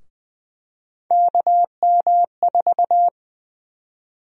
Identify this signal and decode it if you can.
Morse code, heard as a keyed tone, about 20 wpm, 700 Hz. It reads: KM4
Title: USA Call Sign Prefixes - Repeated 2x - 20wpm